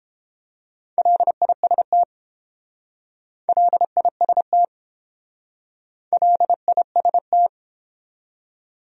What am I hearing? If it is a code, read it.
Morse code, heard as a keyed tone, about 30 wpm, 700 Hz. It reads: LIST LIST LIST